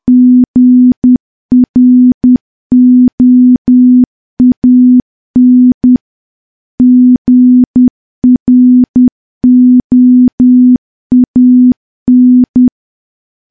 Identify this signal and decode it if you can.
Morse code, heard as a keyed tone, about 10 wpm, 250 Hz. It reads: GROAN GROAN
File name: morse / morse-training-10WPM